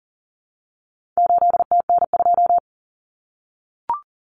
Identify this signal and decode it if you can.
Morse code, heard as a keyed tone, about 40 wpm, 700 Hz. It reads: 8TN2
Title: Common Cut Numbers - 4-Digit 40wpm